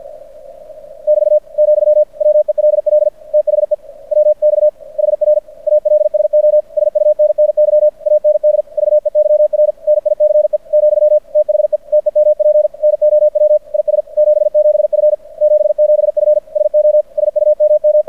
Morse code, heard as a keyed tone, about 53 wpm, 600 Hz. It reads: <BT> , WELL THE WX HR ALSO SUNNY AND VEYR NICE , THE TEMP NOW IS 77F 77F SO SUMM
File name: archived / CRSnet